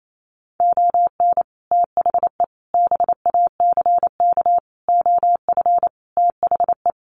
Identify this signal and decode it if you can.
Morse code, heard as a keyed tone, about 28 wpm, 700 Hz. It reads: ON THE BACK OF THE
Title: Sets of 5 Words 28wpm